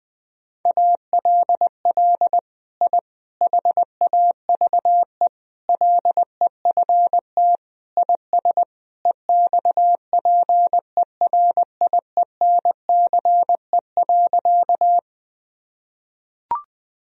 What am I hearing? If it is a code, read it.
Morse code, heard as a keyed tone, about 21 wpm, 700 Hz. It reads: ALL I HAVE LEFT IS EXPERIENCE.
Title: Sentences from Top 1000 Words - Review Part 3 - 20wpm